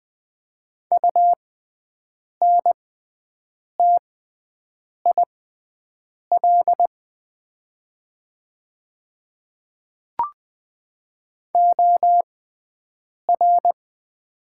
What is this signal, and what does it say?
Morse code, heard as a keyed tone, about 20 wpm, spaced as for 7 wpm, 700 Hz. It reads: UNTIL OR
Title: 6x Character Spacing - Top 500 Words - 20wpm